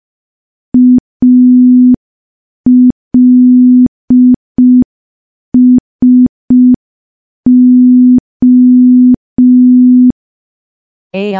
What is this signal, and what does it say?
Morse code, heard as a keyed tone, about 5 wpm, 250 Hz. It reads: ALSO
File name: morse / morse-training-05WPM